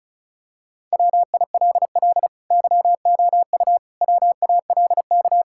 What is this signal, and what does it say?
Morse code, heard as a keyed tone, about 36 wpm, 700 Hz. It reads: WILL YOU WALK
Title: Sentences from Top 1000 Words - Review Part 6 - 35wpm